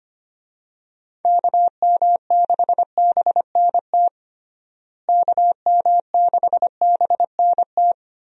Morse code, heard as a keyed tone, about 25 wpm, 700 Hz. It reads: KM6BNT KM6BNT
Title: USA Call Signs - 6 Characters - Repeated 2x - 25wpm